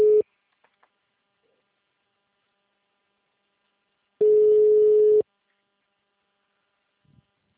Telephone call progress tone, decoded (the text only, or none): ringback tone